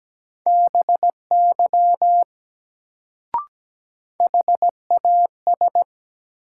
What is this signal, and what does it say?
Morse code, heard as a keyed tone, about 17 wpm, 700 Hz. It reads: BY HAS
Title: Sets of 2 Words - New Word Drill - 17wpm